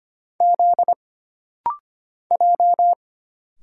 Morse code, heard as a keyed tone, about 25 wpm, 700 Hz. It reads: Z J
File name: Single Character (Letter:Number:Punctuation:Prosign) - Rapid-Fire - 25wpm